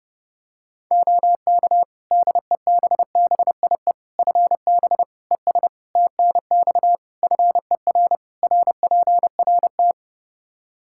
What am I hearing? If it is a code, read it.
Morse code, heard as a keyed tone, about 30 wpm, 700 Hz. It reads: OK DEBBIE FB ES TNX FER RPRT
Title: Practice QSO - Exchange 2 - Info - 30wpm